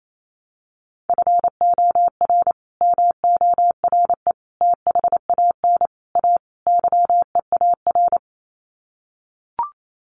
Morse code, heard as a keyed tone, about 28 wpm, 700 Hz. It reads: FOR MORE THAN A YEAR
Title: Sets of 5 Words 28wpm